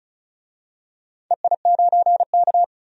Morse code, heard as a keyed tone, about 34 wpm, 700 Hz. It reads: EI9K
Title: DXCC-100 Call Signs - 4 Characters - 35wpm